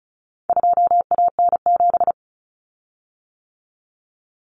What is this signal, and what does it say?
Morse code, heard as a keyed tone, about 34 wpm, 700 Hz. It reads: 2AN7
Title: Common Cut Numbers - 4-Digit 35wpm